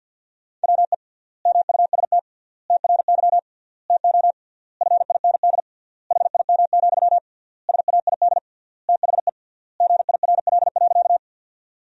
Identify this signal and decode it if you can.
Morse code, heard as a keyed tone, about 50 wpm, 700 Hz. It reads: WE MUST TRY TO FIND HIM, SAID THE GIRL.
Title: 3x Word-Spacing - Sentences from Top 400 Words - 50wpm